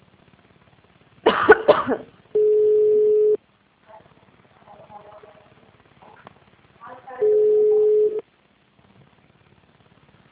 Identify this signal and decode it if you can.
Ringback tone (Europe, 425 Hz), with speech.
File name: out- -2064- - - .12865